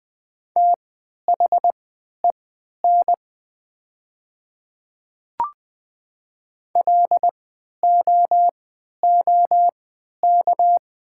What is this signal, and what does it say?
Morse code, heard as a keyed tone, about 20 wpm, spaced as for 11 wpm, 700 Hz. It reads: THEN LOOK
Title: 3x Character Spacing - Top 100 Words - 20wpm